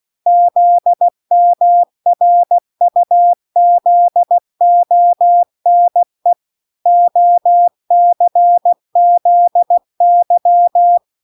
Morse code, heard as a keyed tone, about 16 wpm, 700 Hz. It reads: ZMRUZONE OCZY